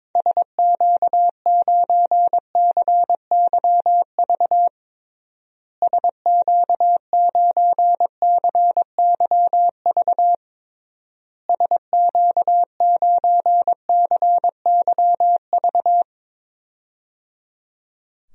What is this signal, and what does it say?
Morse code, heard as a keyed tone, about 22 wpm, 700 Hz. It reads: SQ9CYV SQ9CYV SQ9CYV
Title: DXCC-100 Call Signs - 6 Characters - Repeated 3x - 22wpm